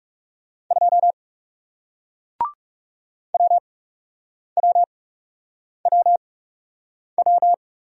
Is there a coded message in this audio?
Yes: Morse code from 0.7 to 7.5 s, heard as a keyed tone, about 40 wpm, spaced as for 8 wpm, 700 Hz. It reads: 2 WWWW